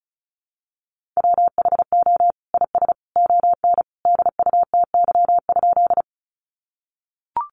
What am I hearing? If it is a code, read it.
Morse code, heard as a keyed tone, about 34 wpm, 700 Hz. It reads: WHO IS ON DUTY?
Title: Sentences from Top 1500 Words 35wpm